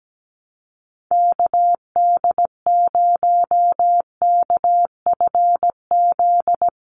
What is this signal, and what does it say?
Morse code, heard as a keyed tone, about 17 wpm, 700 Hz. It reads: KD0KFZ
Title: USA Call Signs 17wpm